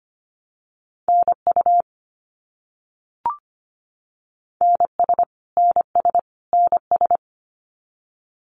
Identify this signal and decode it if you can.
Morse code, heard as a keyed tone, about 25 wpm, 700 Hz. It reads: NU NS NS NS
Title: Canadian Provinces and Territories - Repeated 3x - 25wpm